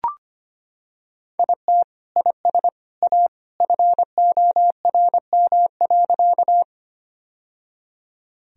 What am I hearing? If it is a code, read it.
Morse code, heard as a keyed tone, about 25 wpm, 700 Hz. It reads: IT IS A FORM.
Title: Sentences from Top 1000 Words - Review Part 1 - 25wpm